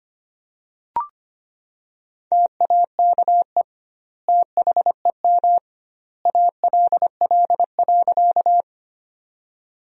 Morse code, heard as a keyed tone, about 25 wpm, 700 Hz. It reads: TAKE THEM ALL.
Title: 2x Word-Spacing - Sentences from Top 200 Words - 25wpm